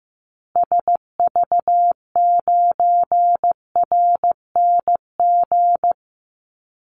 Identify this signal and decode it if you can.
Morse code, heard as a keyed tone, about 15 wpm, 700 Hz. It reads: SV9RNG
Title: Call Signs Repeated 3x - 15wpm